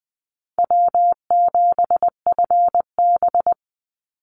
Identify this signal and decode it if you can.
Morse code, heard as a keyed tone, about 20 wpm, 700 Hz. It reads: W7FB
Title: Call Signs Repeated 3x - 20wpm